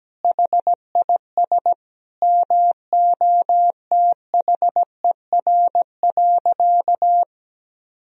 Morse code, heard as a keyed tone, about 17 wpm, 700 Hz. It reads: HIS MOTHER.
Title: Sentences from Top 1000 Words - Review Part 4 - 17wpm